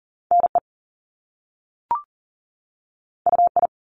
Morse code, heard as a keyed tone, about 38 wpm, 700 Hz. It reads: NE UI
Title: Top Two Letter Combinations in Words 40wpm